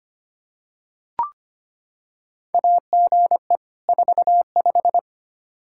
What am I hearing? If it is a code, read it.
Morse code, heard as a keyed tone, about 25 wpm, 700 Hz. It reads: AGE 45
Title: Practice QSO - Exchange 3 - More Info - 25wpm